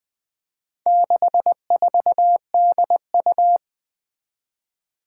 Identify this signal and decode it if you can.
Morse code, heard as a keyed tone, about 20 wpm, 700 Hz. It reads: 64DU
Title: Four Challenging Letter-Number Combinations 20wpm